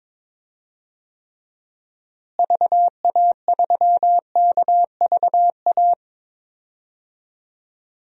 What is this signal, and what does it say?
Morse code, heard as a keyed tone, about 22 wpm, 700 Hz. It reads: VA3KVA
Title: DXCC-100 Call Signs - 6 Characters - Repeated 2x - 22wpm